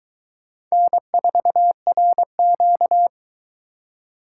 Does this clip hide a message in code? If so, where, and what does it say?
Yes: Morse code from 0.7 to 3.1 s, heard as a keyed tone, about 22 wpm, 700 Hz. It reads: N4RQ